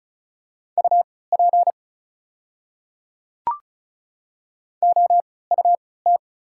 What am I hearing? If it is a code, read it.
Morse code, heard as a keyed tone, about 36 wpm, spaced as for 20 wpm, 700 Hz. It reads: UP OUT
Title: 3x Character Spacing - Top 500 Words - 35wpm